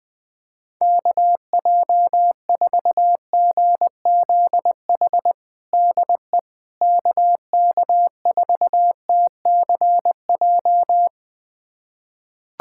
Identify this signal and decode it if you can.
Morse code, heard as a keyed tone, about 20 wpm, 700 Hz. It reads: KJ4GZH DE KK4TCJ
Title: USA Callsign DE Callsign - 6 Characters - 20wpm